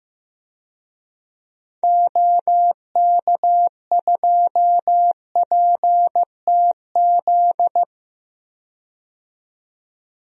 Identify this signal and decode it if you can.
Morse code, heard as a keyed tone, about 15 wpm, 700 Hz. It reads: OK2PTZ
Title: DXCC-100 Call Signs - 6 Characters - 15wpm